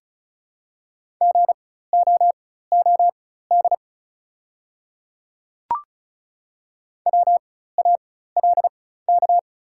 Morse code, heard as a keyed tone, about 34 wpm, spaced as for 16 wpm, 690 Hz. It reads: GOOD WALK
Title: 4x Character Spacing - Top 300 Words - 35wpm